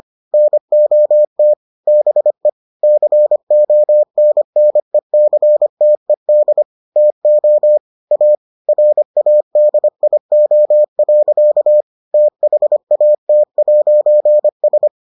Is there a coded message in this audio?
Yes: Morse code from 0.3 to 14.9 s, heard as a keyed tone, about 25 wpm, 600 Hz. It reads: NOT BE CONNECTED TO A RADIO. THAT'S